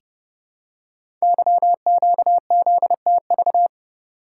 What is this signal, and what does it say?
Morse code, heard as a keyed tone, about 30 wpm, 700 Hz. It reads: YQZTV
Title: Code Groups - 5 Letter:Numbers - 30wpm